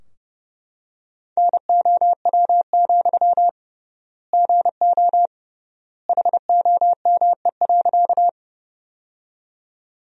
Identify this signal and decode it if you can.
Morse code, heard as a keyed tone, about 30 wpm, 700 Hz. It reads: NOW, GO HOME.
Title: 3x Word-Spacing - Sentences from Top 200 Words - 30wpm